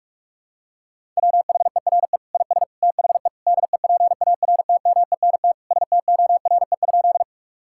Morse code, heard as a keyed tone, about 46 wpm, 700 Hz. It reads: WHERE IS THE DEPARTMENT STORE?
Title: Sentences from Top 1000 Words - Review Part 5 - 45wpm